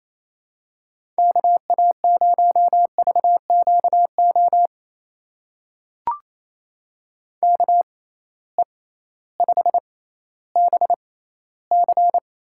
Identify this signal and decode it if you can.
Morse code, heard as a keyed tone, about 28 wpm, spaced as for 15 wpm, 700 Hz. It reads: KA0VQO KE5BC
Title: USA Call Signs - 6 Characters - 6x Character Spacing - 28wpm